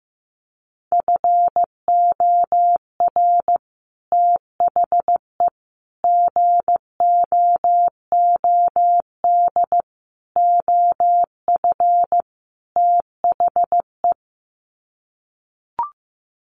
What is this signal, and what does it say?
Morse code, heard as a keyed tone, about 15 wpm, 700 Hz. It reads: FOR THE GOOD OF THE
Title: Sets of 5 Words Encore2 15wpm